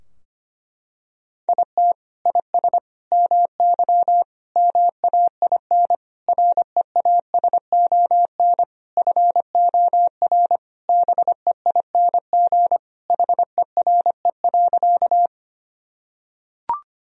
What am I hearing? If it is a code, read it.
Morse code, heard as a keyed tone, about 25 wpm, 700 Hz. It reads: IT IS MY MAIN REASON FOR BEING HERE.
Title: Sentences from Top 1000 Words - Review Part 2 - 25wpm